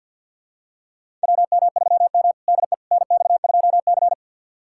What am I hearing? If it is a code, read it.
Morse code, heard as a keyed tone, about 48 wpm, 690 Hz. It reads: WM3M DE NX2C
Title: USA Callsign DE Callsign - 4 Characters - 50wpm